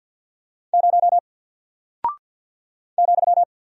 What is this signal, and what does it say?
Morse code, heard as a keyed tone, about 48 wpm, 700 Hz. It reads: 0 ,
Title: Single Character (Letter:Number:Punctuation:Prosign) Alt 50wpm